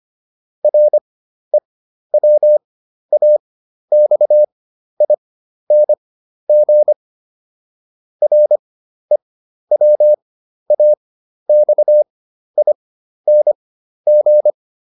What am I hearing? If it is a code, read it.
Morse code, heard as a keyed tone, about 25 wpm, spaced as for 12 wpm, 600 Hz. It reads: REWAXING REWAXING
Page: fundamental-practice-files